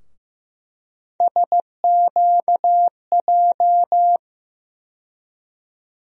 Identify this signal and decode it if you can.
Morse code, heard as a keyed tone, about 15 wpm, 700 Hz. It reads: SQJ
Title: USA Call Sign Postfixes - 3 Letters 15wpm